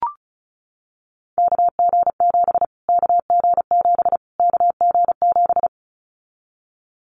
Morse code, heard as a keyed tone, about 34 wpm, 700 Hz. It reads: KG7 KG7 KG7
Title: Callsign Prefixes for Top 25 DXCC - Repeated 3x - 35wpm